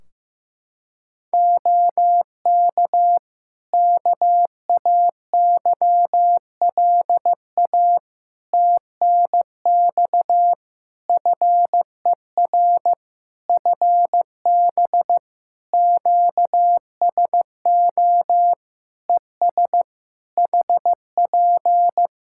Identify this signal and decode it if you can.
Morse code, heard as a keyed tone, about 15 wpm, 700 Hz. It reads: OK KAYLA TNX FER FB QSO ES HP
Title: K1USN SST QSO Practice 15wpm